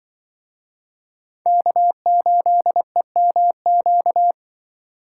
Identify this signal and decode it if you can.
Morse code, heard as a keyed tone, about 24 wpm, 700 Hz. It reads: K8EMQ
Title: Speed Racing - USA 1x3 Call Signs 20wpm